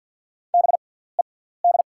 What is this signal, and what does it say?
Morse code, heard as a keyed tone, about 54 wpm, spaced as for 17 wpm, 690 Hz. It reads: BED
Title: 6x Character Spacing - Top 500 Words - 50wpm